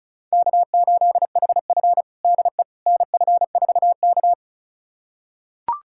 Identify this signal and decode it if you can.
Morse code, heard as a keyed tone, about 36 wpm, 700 Hz. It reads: K8HF DE NF4K
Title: USA Callsign DE Callsign - 4 Characters - 35wpm